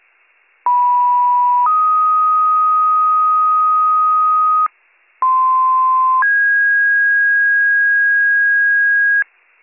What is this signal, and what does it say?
Two-tone page, heard as a one-second tone, then a three-second tone, about 980 then 1250 Hz, about 1040 then 1650 Hz, with noise. It is heard from 0.7 to 9.2 s.